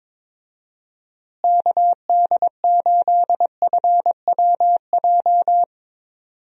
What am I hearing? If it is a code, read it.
Morse code, heard as a keyed tone, about 22 wpm, 700 Hz. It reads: KD8FWJ
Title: USA Call Signs - 6 Characters - Repeated 3x - 22wpm